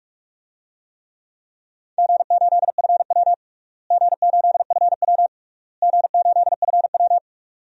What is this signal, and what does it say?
Morse code, heard as a keyed tone, about 43 wpm, 700 Hz. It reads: G8FW G8FW G8FW
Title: DXCC-100 Call Signs - 4 Characters - Repeated 3x - 45wpm